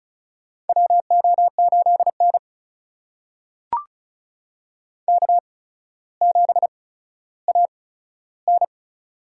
Morse code, heard as a keyed tone, about 36 wpm, spaced as for 15 wpm, 700 Hz. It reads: WO8N K7AN
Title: USA Call Signs - 4 Characters - 8x Character Spacing - 35wpm